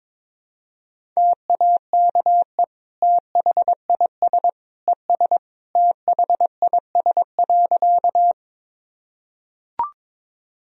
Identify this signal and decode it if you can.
Morse code, heard as a keyed tone, about 22 wpm, 700 Hz. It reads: TAKE THIS ES THIS.
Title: Sentences from Top 200 Words and Abbreviations 22wpm